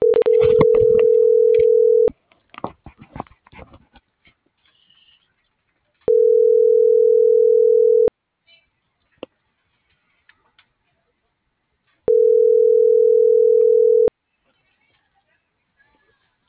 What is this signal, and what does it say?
Ringback tone (North America), 440 + 480 Hz.